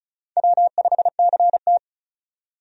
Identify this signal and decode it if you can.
Morse code, heard as a keyed tone, about 36 wpm, 700 Hz. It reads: W5CT
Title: USA Call Signs - 4 Characters - 35wpm